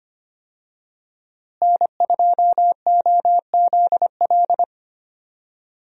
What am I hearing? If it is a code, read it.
Morse code, heard as a keyed tone, about 25 wpm, 700 Hz. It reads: N2OZL
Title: USA Call Signs - 5 Characters - Repeated 3x - 25wpm